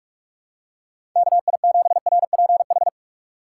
Morse code, heard as a keyed tone, about 42 wpm, 690 Hz. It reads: KI7RPH
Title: USA Call Signs - 6 Characters - 3x Character Spacing - 45wpm